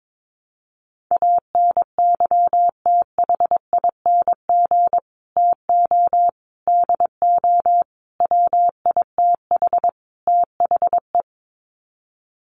Morse code, heard as a keyed tone, about 22 wpm, 700 Hz. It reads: ANYTHING TO DO WITH THE
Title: Sets of 5 Words Encore 22wpm